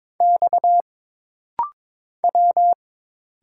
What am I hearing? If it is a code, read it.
Morse code, heard as a keyed tone, about 22 wpm, 700 Hz. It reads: X W
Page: This rapid-fire timing is geared to push your ICR (Instant Character Recognition) skills.